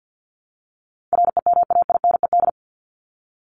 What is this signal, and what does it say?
Morse code, heard as a keyed tone, about 49 wpm, 700 Hz. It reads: REMAINED
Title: ED Suffix - Top 100 Words with Base Words 50wpm